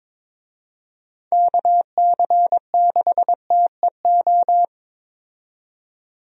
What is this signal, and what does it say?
Morse code, heard as a keyed tone, about 22 wpm, 700 Hz. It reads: KC6TEO
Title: USA Call Signs - 6 Characters - 22wpm